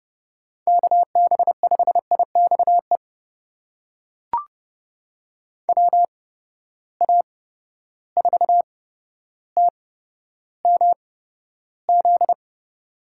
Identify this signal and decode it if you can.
Morse code, heard as a keyed tone, about 30 wpm, spaced as for 13 wpm, 700 Hz. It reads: KB5IXE WA4TMZ
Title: USA Call Signs - 6 Characters - 8x Character Spacing - 30wpm